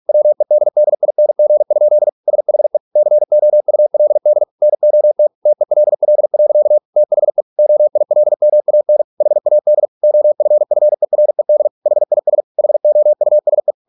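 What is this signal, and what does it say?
Morse code, heard as a keyed tone, about 46 wpm, 600 Hz. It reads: WEDDING? SHE COULD NOT TELL. THE OILMAN HAD OFFERED HIS HOUSE
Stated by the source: Morse practice at 45 wpm, the novel Never by Ken Follett in English, from CW med Gnister.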